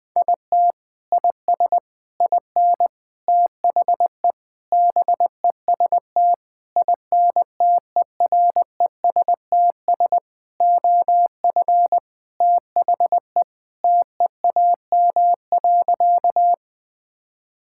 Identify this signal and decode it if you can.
Morse code, heard as a keyed tone, about 20 wpm, 700 Hz. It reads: IT IS IN THE BEST INTERESTS OF THE TEAM.
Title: Sentences with 6-Word Phrases - 20wpm